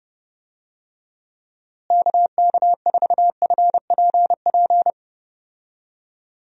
Morse code, heard as a keyed tone, about 30 wpm, 700 Hz. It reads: KK4FPP
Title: USA Call Signs - 6 Characters - Repeated 3x - 30wpm